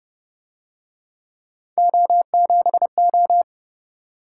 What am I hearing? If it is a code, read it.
Morse code, heard as a keyed tone, about 31 wpm, 700 Hz. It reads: O7O